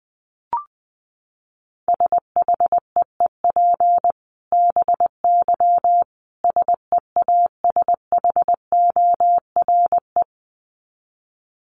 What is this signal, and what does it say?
Morse code, heard as a keyed tone, about 19 wpm, 700 Hz. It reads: SHEEP BY SEASHORE
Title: Dit Challenge 20wpm